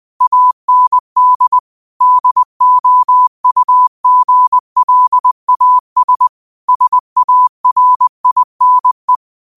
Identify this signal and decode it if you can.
Morse code, heard as a keyed tone, about 20 wpm, 1000 Hz. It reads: AND DOUGLAS SARINE
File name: qotdfiles / 20wpm